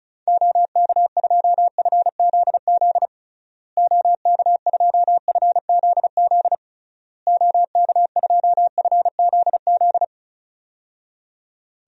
Morse code, heard as a keyed tone, about 36 wpm, 700 Hz. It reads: OK2FZZ OK2FZZ OK2FZZ
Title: DXCC-100 Call Signs - 6 Characters - Repeated 3x - 35wpm